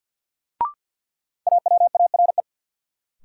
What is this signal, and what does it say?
Morse code, heard as a keyed tone, about 47 wpm, 700 Hz. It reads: AWARE
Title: Sentences from Top 1200 Words - New Word Drill - 50wpm